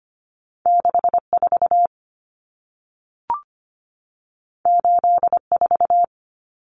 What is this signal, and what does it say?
Morse code, heard as a keyed tone, about 25 wpm, 700 Hz. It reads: 64 84
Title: Numbers - 2-Digit 25wpm